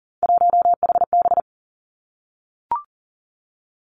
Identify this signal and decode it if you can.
Morse code, heard as a keyed tone, about 41 wpm, 690 Hz. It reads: 1HB
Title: Three Challenging Letter-Number Combinations 40wpm